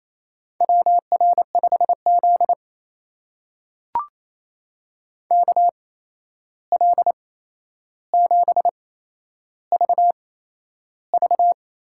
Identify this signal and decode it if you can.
Morse code, heard as a keyed tone, about 28 wpm, spaced as for 11 wpm, 700 Hz. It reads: WR5Z KL7VV
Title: USA Call Signs - 8x Character Spacing - 28wpm